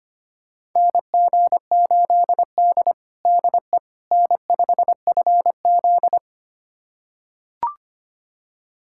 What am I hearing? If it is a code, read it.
Morse code, heard as a keyed tone, about 25 wpm, 700 Hz. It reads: NG8D DE N5FZ
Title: USA Callsign DE Callsign - 4 Characters - 25wpm